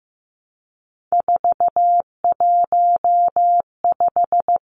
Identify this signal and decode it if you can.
Morse code, heard as a keyed tone, about 15 wpm, 700 Hz. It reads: 415
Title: Numbers - 3-Digit 15wpm